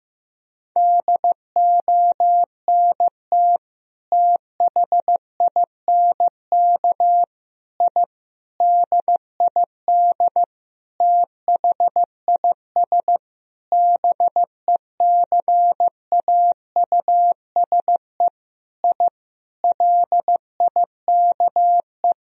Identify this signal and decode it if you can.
Morse code, heard as a keyed tone, about 15 wpm, 700 Hz. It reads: DONT THINK I DID THIS BECAUSE I LIKE
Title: Sentences from Top 1000 Words - Review Part 3 - 15wpm